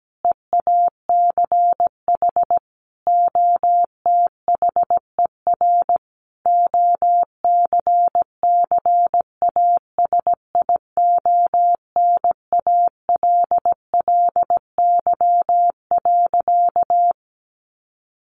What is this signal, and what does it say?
Morse code, heard as a keyed tone, about 17 wpm, 700 Hz. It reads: EACH OTHER OCCASIONALLY.
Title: Sentences from Top 1300 Words 17wpm